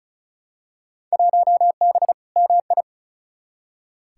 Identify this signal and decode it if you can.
Morse code, heard as a keyed tone, about 36 wpm, 700 Hz. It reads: 1B MI
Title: Field Day Exchanges - 35wpm